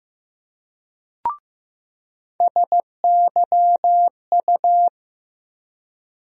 Morse code, heard as a keyed tone, about 15 wpm, 700 Hz. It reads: SYU
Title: USA Call Sign Postfixes - 3 Letters 15wpm